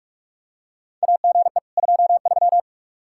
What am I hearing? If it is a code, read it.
Morse code, heard as a keyed tone, about 46 wpm, 700 Hz. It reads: AGE 23
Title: K1USN SST QSO Practice 45wpm